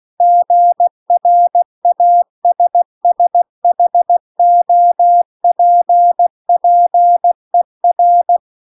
Morse code, heard as a keyed tone, about 16 wpm, 700 Hz. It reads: GRASSHOPPER